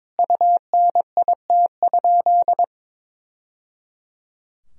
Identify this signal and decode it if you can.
Morse code, heard as a keyed tone, about 22 wpm, 700 Hz. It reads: UNIT?
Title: Sentences from Top 1000 Words - Review Part 2 - 22wpm